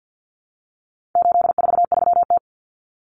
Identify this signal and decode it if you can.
Morse code, heard as a keyed tone, about 50 wpm, 700 Hz. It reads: 843T
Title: Common Cut Numbers - 4-Digit 50wpm